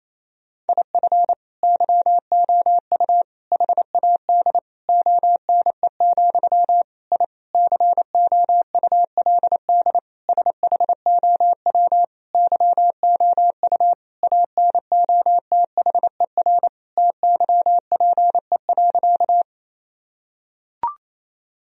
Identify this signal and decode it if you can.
Morse code, heard as a keyed tone, about 28 wpm, 700 Hz. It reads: IF YOU HAD ONE, I COULD SHOW YOU ANOTHER TYPE.
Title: Sentences from Top 1000 Words - Review Part 3 - 28wpm